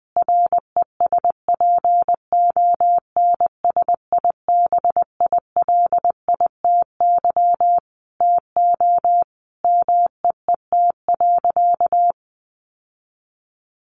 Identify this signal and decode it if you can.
Morse code, heard as a keyed tone, about 20 wpm, 700 Hz. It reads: RESPONSIBILITY TO MEET.
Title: Sentences from Top 1200 Words 20wpm